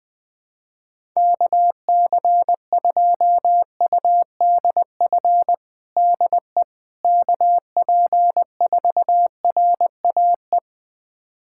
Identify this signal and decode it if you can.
Morse code, heard as a keyed tone, about 21 wpm, 700 Hz. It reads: KC2UDF DE KP4RAE
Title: USA Callsign DE Callsign - 6 Characters - 20wpm